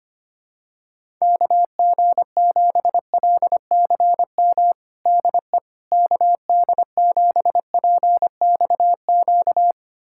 Morse code, heard as a keyed tone, about 25 wpm, 700 Hz. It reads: KG7LCM DE KD7PXQ
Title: USA Callsign DE Callsign - 6 Characters - 25wpm